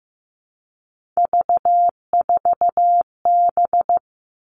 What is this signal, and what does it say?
Morse code, heard as a keyed tone, about 15 wpm, 700 Hz. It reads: V4B
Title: All Cut Numbers - 3-Digit 15wpm